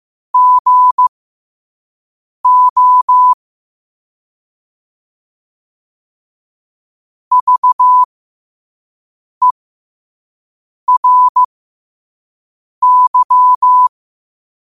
Morse code, heard as a keyed tone, about 15 wpm, spaced as for 5 wpm, 1000 Hz. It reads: GO VERY
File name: qotdfiles / 5wpm